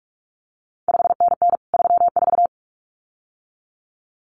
Morse code, heard as a keyed tone, about 46 wpm, 700 Hz. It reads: 5NN 34
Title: CQ WW DX Contest Exchange Encore - 45wpm